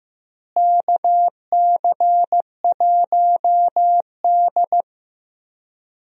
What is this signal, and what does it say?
Morse code, heard as a keyed tone, about 15 wpm, 700 Hz. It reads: KC1D
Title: USA Call Signs - 4 Characters - Repeated 2x - 15wpm